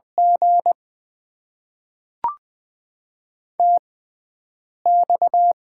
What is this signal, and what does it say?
Morse code, heard as a keyed tone, about 20 wpm, spaced as for 7 wpm, 700 Hz. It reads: G TX
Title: Top 90 QSO Elements - 6x Character Spacing - 20wpm